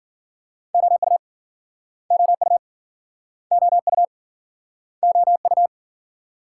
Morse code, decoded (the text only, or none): OU OU OU OU